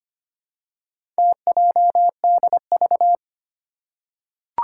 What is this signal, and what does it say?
Morse code, heard as a keyed tone, about 25 wpm, 700 Hz. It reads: TJDV
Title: Four Challenging Letter-Number Combinations 25wpm